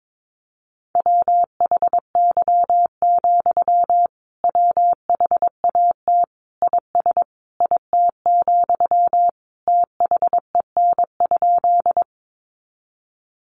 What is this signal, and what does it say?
Morse code, heard as a keyed tone, about 22 wpm, 700 Hz. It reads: WHY, WHAT IS IT, THEN?
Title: Sentences from Top 200 Words 22wpm